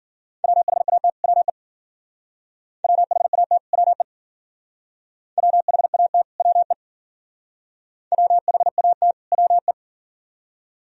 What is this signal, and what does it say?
Morse code, heard as a keyed tone, about 49 wpm, 700 Hz. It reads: WHAT WE WHAT WE WHAT WE WHAT WE
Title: Speed-Racing - Sets of 2 Words 40wpm